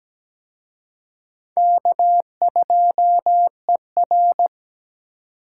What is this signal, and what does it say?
Morse code, heard as a keyed tone, about 20 wpm, 700 Hz. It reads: K2ER
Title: Speed Racing - USA Call Signs 17wpm